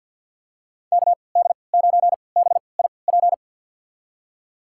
Morse code, heard as a keyed tone, about 52 wpm, spaced as for 28 wpm, 700 Hz. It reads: KD9BIP
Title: USA Call Signs - 6 Characters - 3x Character Spacing - 50wpm